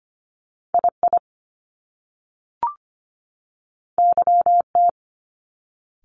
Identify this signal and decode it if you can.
Morse code, heard as a keyed tone, about 25 wpm, 700 Hz. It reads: II YT
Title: Top Two Letter Combinations in Words 25wpm